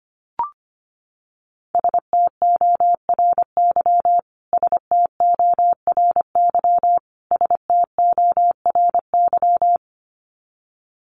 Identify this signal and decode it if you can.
Morse code, heard as a keyed tone, about 25 wpm, 700 Hz. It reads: STORY STORY STORY
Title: Top 500 Words Repeated 25wpm